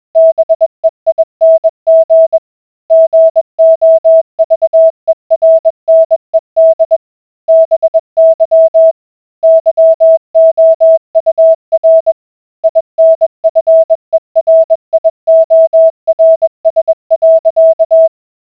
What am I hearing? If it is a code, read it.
Morse code, heard as a keyed tone, about 20 wpm, 650 Hz. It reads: BEING GOVERNED BY YOUR INFERIORS.